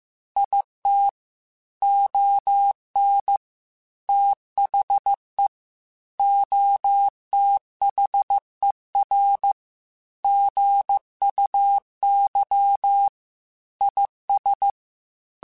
Morse code, heard as a keyed tone, about 15 wpm, 800 Hz. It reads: IT ON THE OTHER GUY IS
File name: Chuckles / Chuckles-mp3s / HumanErr @13